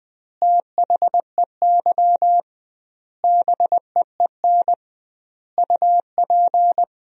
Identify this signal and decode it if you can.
Morse code, heard as a keyed tone, about 20 wpm, 700 Hz. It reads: THEY BEEN UP
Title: 2x Word-Spacing - Sentences from Top 100 Words - 20wpm